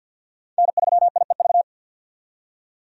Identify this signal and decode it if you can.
Morse code, heard as a keyed tone, about 47 wpm, 700 Hz. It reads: N3IEV
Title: USA Call Signs - 5 Characters - 50wpm